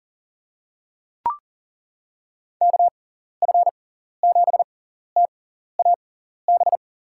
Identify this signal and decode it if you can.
Morse code, heard as a keyed tone, about 40 wpm, spaced as for 14 wpm, 700 Hz. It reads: KF7TAB
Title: USA Call Signs - 6 Characters - 6x Character Spacing - 40wpm